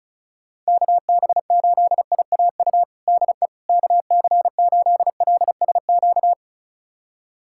Morse code, heard as a keyed tone, about 36 wpm, 700 Hz. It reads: KB8IAU DE KC8LSQ
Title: USA Callsign DE Callsign - 6 Characters - 35wpm